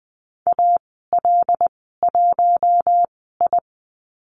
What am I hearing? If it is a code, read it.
Morse code, heard as a keyed tone, about 20 wpm, spaced as for 14 wpm, 700 Hz. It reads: AL1I
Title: 140 Lesson 23g - Letter f - Call Signs 20wpm